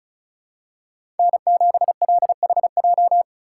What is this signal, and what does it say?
Morse code, heard as a keyed tone, about 36 wpm, 700 Hz. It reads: N7LHJ
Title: USA Call Signs - 5 Characters - 2x Character Spacing - 35wpm